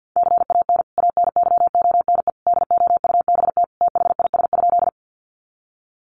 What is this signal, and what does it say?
Morse code, heard as a keyed tone, about 50 wpm, 700 Hz. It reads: CAN ANYONE DOUBT THIS?
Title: Sentences from Top 1500 Words 50wpm